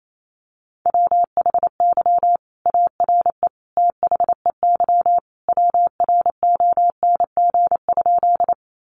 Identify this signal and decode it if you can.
Morse code, heard as a keyed tone, about 28 wpm, 690 Hz. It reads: WHY ARE THEY WRONG?
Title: Sentences from Top 900 Words 28wpm